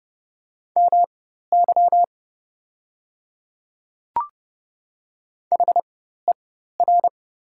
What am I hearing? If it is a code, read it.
Morse code, heard as a keyed tone, about 30 wpm, spaced as for 14 wpm, 700 Hz. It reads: MY HER
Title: 4x Character Spacing - Top 300 Words - 30wpm